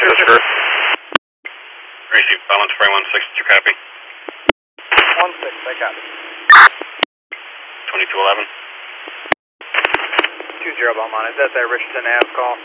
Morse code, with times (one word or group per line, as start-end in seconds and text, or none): none